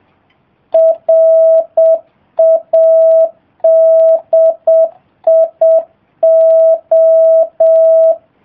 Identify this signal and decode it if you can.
Morse code, heard as a keyed tone, about 7 wpm, 650 Hz. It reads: RADIO